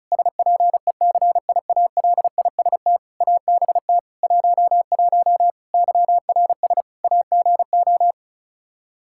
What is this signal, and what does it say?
Morse code, heard as a keyed tone, about 34 wpm, 700 Hz. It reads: SPECIALIST ABT 11 YRS AGO
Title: Practice QSO - Exchange 3 - More Info - 35wpm